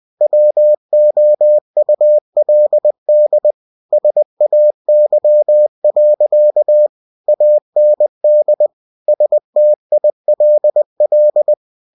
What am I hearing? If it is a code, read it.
Morse code, heard as a keyed tone, about 20 wpm, 600 Hz. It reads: WOULD SAY. AND STILL